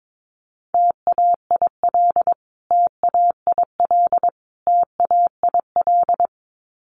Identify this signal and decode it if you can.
Morse code, heard as a keyed tone, about 22 wpm, 700 Hz. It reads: TAIL TAIL TAIL
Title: Top 500 Words Repeated 22wpm